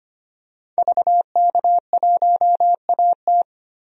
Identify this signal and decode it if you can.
Morse code, heard as a keyed tone, about 25 wpm, 700 Hz. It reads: VK1AT
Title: DXCC-100 Call Signs - 5 Characters - 25wpm